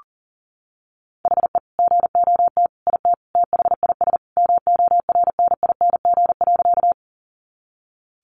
Morse code, heard as a keyed tone, about 40 wpm, 700 Hz. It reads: HE GOT IT THIS MORNING.
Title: Sentences from Top 400 Words 40wpm